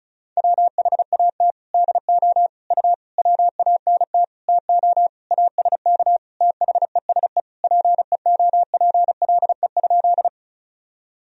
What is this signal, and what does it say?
Morse code, heard as a keyed tone, about 34 wpm, 690 Hz. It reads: WHAT DO U WANT TO ASK THESE PEOPLE?
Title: Sentences from Top 200 Words and Abbreviations 35wpm